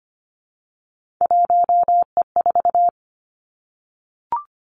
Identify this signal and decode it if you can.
Morse code, heard as a keyed tone, about 25 wpm, 700 Hz. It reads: 1E4
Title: Three Challenging Letter-Number Combinations 25wpm